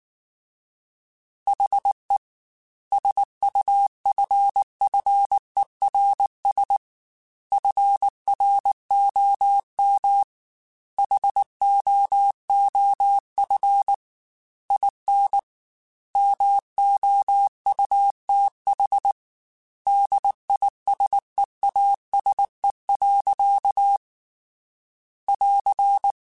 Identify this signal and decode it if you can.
Morse code, heard as a keyed tone, about 19 wpm, 800 Hz. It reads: HE SUFFERS FROM HOOF IN MOUTH DISEASE. <AR>